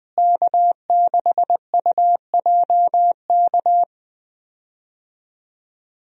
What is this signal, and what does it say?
Morse code, heard as a keyed tone, about 20 wpm, 700 Hz. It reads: K6UJK
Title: USA Call Signs - 5 Characters - Repeated 2x - 20wpm